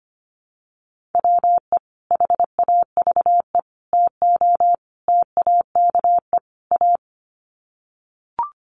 Morse code, heard as a keyed tone, about 25 wpm, 700 Hz. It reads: WE HAVE TO TAKE A
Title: Sets of 5 Words Encore 25wpm